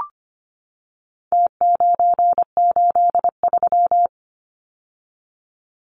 Morse code, heard as a keyed tone, about 25 wpm, 700 Hz. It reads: T983
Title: Numbers - Contest Style - 4-Digit 25wpm